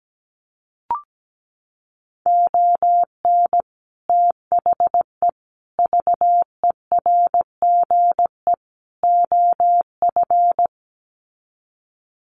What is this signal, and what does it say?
Morse code, heard as a keyed tone, about 17 wpm, 700 Hz. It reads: ON THE VERGE OF
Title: Sets of 4 Words 17wpm